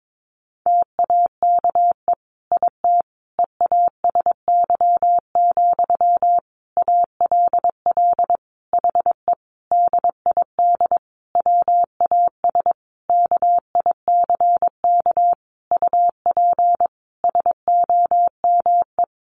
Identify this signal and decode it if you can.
Morse code, heard as a keyed tone, about 22 wpm, 700 Hz. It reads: TAKE IT EASY, ALL HE DID WAS KICK UP SOME
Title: A Sound Of Thunder 22wpm